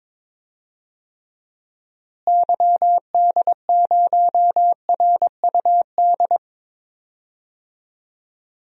Morse code, heard as a keyed tone, about 22 wpm, 700 Hz. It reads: YD0RUD
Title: DXCC-100 Call Signs - 6 Characters - Repeated 2x - 22wpm